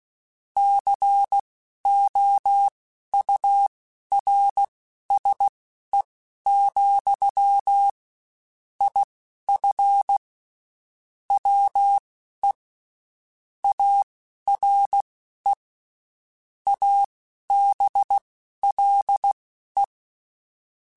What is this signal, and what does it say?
Morse code, heard as a keyed tone, about 16 wpm, spaced as for 11 wpm, 800 Hz. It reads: COURSE, IF WE ARE ABLE
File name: (spaced) / QSOs Spaced-mp3s / WA7QOX 1 @15 sp